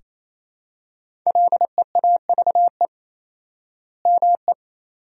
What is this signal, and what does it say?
Morse code, heard as a keyed tone, about 27 wpm, 700 Hz. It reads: LEAVE ME
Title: 4x Word-Spacing - Sentences from Top 400 Words - 28wpm